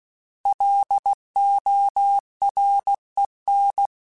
Morse code, heard as a keyed tone, about 16 wpm, 800 Hz. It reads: LOREN